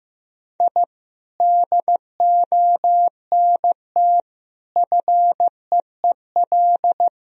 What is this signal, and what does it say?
Morse code, heard as a keyed tone, about 15 wpm, 700 Hz. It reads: I DONT FEEL
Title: Sentences from Top 1000 Words - Review Part 2 - 15wpm